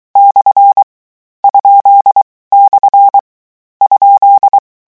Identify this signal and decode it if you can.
Morse code, heard as a keyed tone, about 23 wpm, 800 Hz. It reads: / ? / ?
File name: morse mp3 / lecon13-vitesse 22